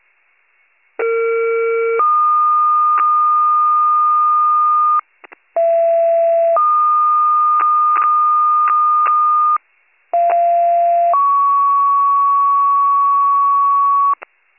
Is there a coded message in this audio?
Yes: two-tone page from 1.0 to 14.1 s, heard as a one-second tone, then a three-second tone, about 460 then 1220 Hz, about 670 then 1220 Hz, about 690 then 1120 Hz.